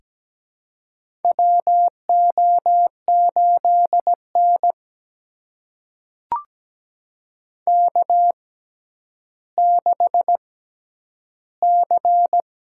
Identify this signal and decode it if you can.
Morse code, heard as a keyed tone, about 17 wpm, spaced as for 10 wpm, 700 Hz. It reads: WO8N K6C
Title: USA Call Signs - 4 Characters - 6x Character Spacing - 17wpm